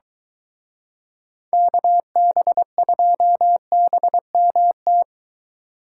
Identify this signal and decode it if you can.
Morse code, heard as a keyed tone, about 23 wpm, 700 Hz. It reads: KB2BMT